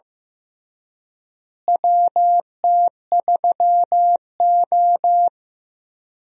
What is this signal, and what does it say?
Morse code, heard as a keyed tone, about 15 wpm, 700 Hz. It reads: WT3O
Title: USA Call Signs - 4 Characters - Repeated 2x - 15wpm